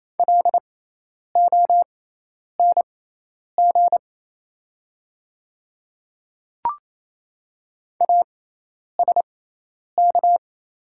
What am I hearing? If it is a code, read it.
Morse code, heard as a keyed tone, about 27 wpm, spaced as for 10 wpm, 690 Hz. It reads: LONG ASK
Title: 6x Character Spacing - Top 300 Words - 28wpm